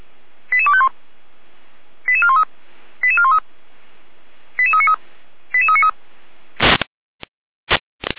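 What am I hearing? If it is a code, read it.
Five-tone selective call (ZVEI), heard as quick runs of five tones, tones of about 70 ms. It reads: 88251 88413 88413 88393 88393
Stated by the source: ZVEI five-tone call received on SDR.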